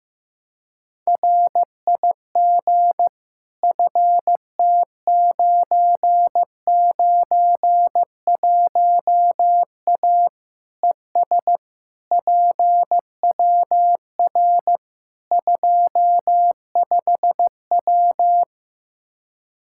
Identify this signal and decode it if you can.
Morse code, heard as a keyed tone, about 15 wpm, 700 Hz. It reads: RIG FT991A ES PWR 25W
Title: Practice QSO - Exchange 2 - Info - 15wpm